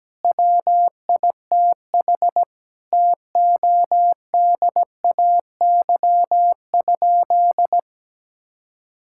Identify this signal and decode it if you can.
Morse code, heard as a keyed tone, about 17 wpm, 700 Hz. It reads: WITH TODAY?
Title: Sentences from Top 1000 Words - Review Part 6 - 17wpm